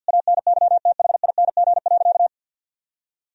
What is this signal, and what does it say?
Morse code, heard as a keyed tone, about 51 wpm, 700 Hz. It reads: ANYTHING.
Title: Sentences from Top 1000 Words - Review Part 2 - 50wpm